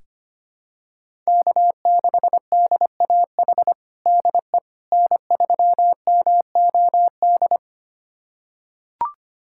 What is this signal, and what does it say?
Morse code, heard as a keyed tone, about 25 wpm, 700 Hz. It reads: K6DAH DE N3MOD
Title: USA Callsign DE Callsign - 5 Characters - 25wpm